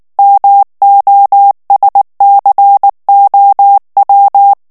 Morse code, heard as a keyed tone, about 19 wpm, 800 Hz. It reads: MOSCOW
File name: ANGLAIS texte clair 2 vitesse 18 mots minute